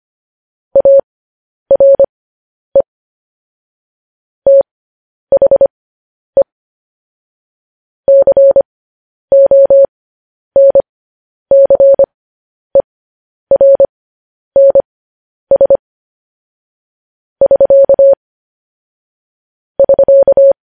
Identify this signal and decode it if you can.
Morse code, heard as a keyed tone, about 25 wpm, spaced as for 10 wpm, 550 Hz. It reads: ARE THE CONCERNS <SK> <SK>
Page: news headlines in morse code at 10 wpm